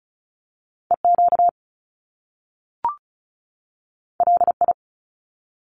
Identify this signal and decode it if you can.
Morse code, heard as a keyed tone, about 36 wpm, 700 Hz. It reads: EQ LI
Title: Top Two Letter Combinations in Words 35wpm